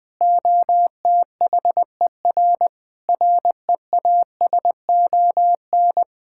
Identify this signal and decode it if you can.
Morse code, heard as a keyed tone, about 20 wpm, 700 Hz. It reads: OTHER REASON
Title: Sentences with 7-Word Phrases - 20wpm